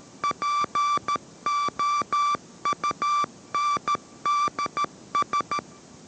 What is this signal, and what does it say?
Morse code, heard as a keyed tone, about 13 wpm, 1170 Hz. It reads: POUNDS